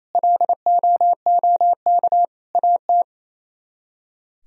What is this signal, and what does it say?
Morse code, heard as a keyed tone, about 35 wpm, 700 Hz. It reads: LOOK AT